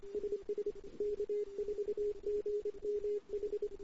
Morse code, heard as a keyed tone, about 26 wpm, 410 Hz. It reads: SS K4GMH
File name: CW / 2008-ss-cw-k5tr-op-n5rz -08:00